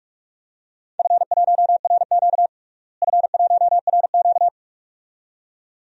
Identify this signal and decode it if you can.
Morse code, heard as a keyed tone, about 45 wpm, 700 Hz. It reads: F1RQ F1RQ
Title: DXCC-100 Call Signs - 4 Characters - Repeated 2x - 45wpm